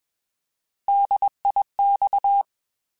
Morse code, heard as a keyed tone, about 21 wpm, 800 Hz. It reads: DIX
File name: Alphabetical / Alphabetical mp3s / Ds @20